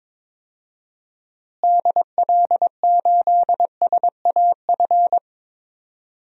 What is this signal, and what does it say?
Morse code, heard as a keyed tone, about 22 wpm, 700 Hz. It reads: DL8SAF
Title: DXCC-100 Call Signs - 6 Characters - Repeated 3x - 22wpm